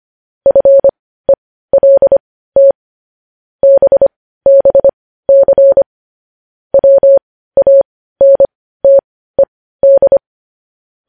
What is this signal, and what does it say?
Morse code, heard as a keyed tone, about 25 wpm, spaced as for 15 wpm, 550 Hz. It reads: FELT BBC WANTED
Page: News headlines in morse code at 15 wpm